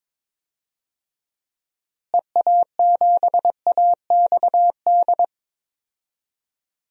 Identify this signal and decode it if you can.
Morse code, heard as a keyed tone, about 22 wpm, 700 Hz. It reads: EA7AXD
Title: DXCC-100 Call Signs - 6 Characters - Repeated 3x - 22wpm